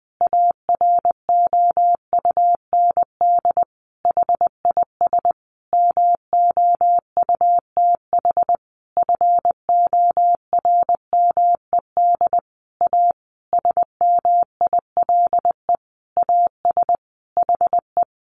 Morse code, heard as a keyed tone, about 20 wpm, 700 Hz. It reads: AROUND HIS MOUTH FORMED A SMILE AS HE
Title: A Sound Of Thunder 20wpm